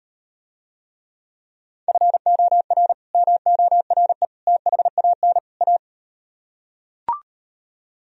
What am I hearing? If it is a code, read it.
Morse code, heard as a keyed tone, about 38 wpm, 700 Hz. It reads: FOR MORE THAN A